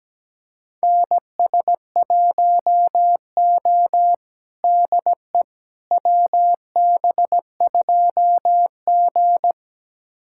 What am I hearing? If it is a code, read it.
Morse code, heard as a keyed tone, about 17 wpm, 700 Hz. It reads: NS1O DE WB2G
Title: USA Callsign DE Callsign - 4 Characters - 17wpm